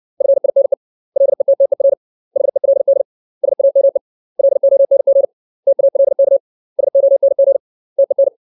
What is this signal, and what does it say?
Morse code, heard as a keyed tone, about 60 wpm, 550 Hz. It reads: FINE LETTER HELD SAME LONG TALK SONG TEN
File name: 632 words / 632 Words 60WPM